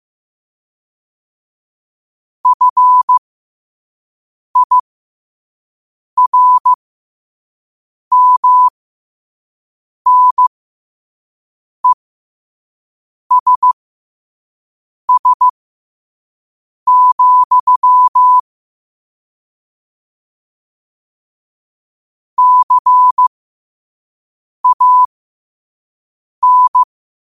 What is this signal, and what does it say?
Morse code, heard as a keyed tone, about 15 wpm, spaced as for 5 wpm, 1000 Hz. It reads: FIRMNESS, CAN
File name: qotdfiles / 5wpm